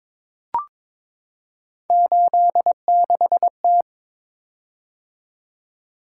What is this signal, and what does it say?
Morse code, heard as a keyed tone, about 22 wpm, 700 Hz. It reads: 86T
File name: practice / 22wpm / Three Letter-Number Combinations - 22wpm